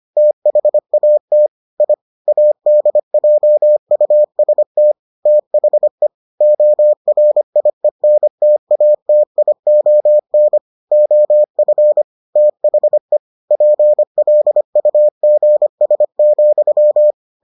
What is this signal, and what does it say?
Morse code, heard as a keyed tone, about 25 wpm, 600 Hz. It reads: THAT I ADJUST THE ORIENTATION OF THE PLUGS,